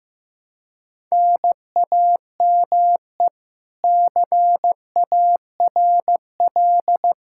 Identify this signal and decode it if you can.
Morse code, heard as a keyed tone, about 15 wpm, 700 Hz. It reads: NAME CARL
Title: K1USN SST QSO Practice 15wpm